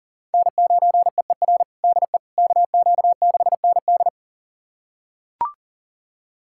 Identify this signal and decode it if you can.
Morse code, heard as a keyed tone, about 40 wpm, 700 Hz. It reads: N9EER DE KQ6ND
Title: USA Callsign DE Callsign - 5 Characters - 40wpm